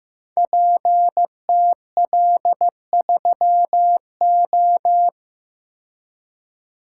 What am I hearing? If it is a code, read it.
Morse code, heard as a keyed tone, about 15 wpm, 700 Hz. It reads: PTL3O
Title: Code Groups - 5 Letter:Numbers - 15wpm